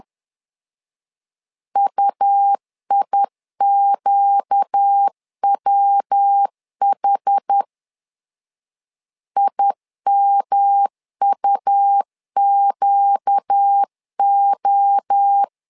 Morse code, heard as a keyed tone, about 10 wpm, 800 Hz. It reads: UIQWH IMUQO
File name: LEZIONI / 4 Q1S3 10wpm 2427